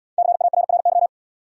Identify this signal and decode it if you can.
Morse code, heard as a keyed tone, about 67 wpm, 700 Hz. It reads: CARRY